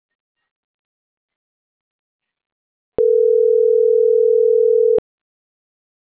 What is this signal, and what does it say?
Ringback tone (North America), 440 + 480 Hz.